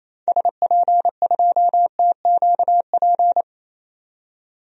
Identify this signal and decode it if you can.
Morse code, heard as a keyed tone, about 28 wpm, 700 Hz. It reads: SP2TQP
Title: DXCC-100 Call Signs - 6 Characters - 28wpm